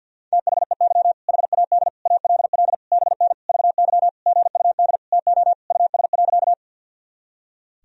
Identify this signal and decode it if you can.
Morse code, heard as a keyed tone, about 47 wpm, 700 Hz. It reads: THEY HAD ALL BN VY GUD TO US.
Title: Sentences from Top 200 Words and Abbreviations 50wpm